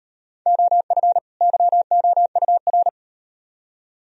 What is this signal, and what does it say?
Morse code, heard as a keyed tone, about 36 wpm, 700 Hz. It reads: OF YOUR